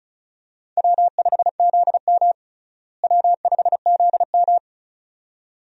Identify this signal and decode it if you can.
Morse code, heard as a keyed tone, about 36 wpm, 700 Hz. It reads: W5ZM W5ZM
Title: USA Call Signs - 4 Characters - Repeated 2x - 35wpm